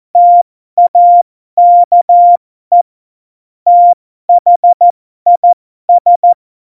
Morse code, heard as a keyed tone, about 14 wpm, spaced as for 12 wpm, 700 Hz. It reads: TAKE THIS
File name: Morsekurs / Witze